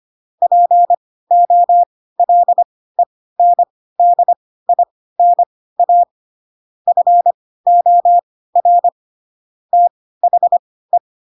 Morse code, heard as a keyed tone, about 25 wpm, spaced as for 16 wpm, 700 Hz. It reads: POLENDINA FOR THE
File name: Pinocchio / Pinocchio ch2 10 25 ik1hge